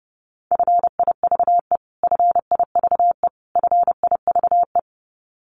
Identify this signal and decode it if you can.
Morse code, heard as a keyed tone, about 30 wpm, 700 Hz. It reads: FIVE FIVE FIVE
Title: Top 500 Words Repeated 30wpm